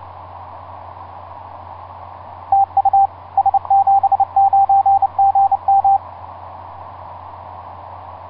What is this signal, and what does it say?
Morse code, heard as a keyed tone, about 28 wpm, 790 Hz, with radio noise. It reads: TU S79GM
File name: s79gm s59z etc 10m cw 30mar2010 1126